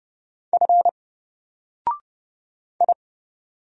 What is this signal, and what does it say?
Morse code, heard as a keyed tone, about 28 wpm, 700 Hz. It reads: F I
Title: Single Character (Letter:Number:Punctuation:Prosign) 30wpm